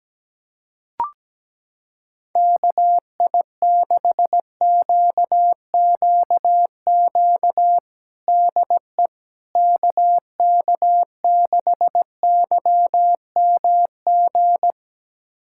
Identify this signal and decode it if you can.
Morse code, heard as a keyed tone, about 17 wpm, 700 Hz. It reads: KI6QQQ DE KK6YMG
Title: USA Callsign DE Callsign 17wpm